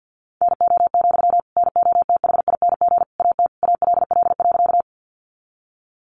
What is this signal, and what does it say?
Morse code, heard as a keyed tone, about 50 wpm, 700 Hz. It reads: NO, NOTHING AT ALL.
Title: Sentences from Top 500 Words 50wpm